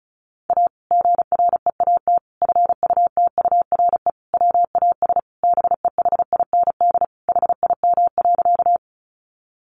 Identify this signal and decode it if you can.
Morse code, heard as a keyed tone, about 34 wpm, 700 Hz. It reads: A GREAT FUTURE WAS BEHIND HIM.
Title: Sentences from Top 700 Words 35wpm